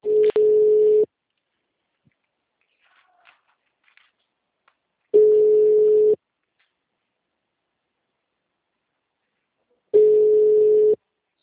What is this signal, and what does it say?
Ringback tone (Europe, 425 Hz).